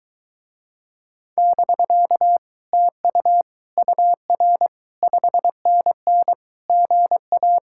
Morse code, heard as a keyed tone, about 23 wpm, 700 Hz. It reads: <BK> TU UR 5NN GA